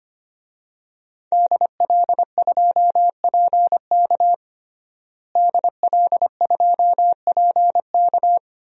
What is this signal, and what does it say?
Morse code, heard as a keyed tone, about 25 wpm, 700 Hz. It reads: DL2PK DL2PK
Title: DXCC-100 Call Signs - 5 Characters - Repeated 3x - 25wpm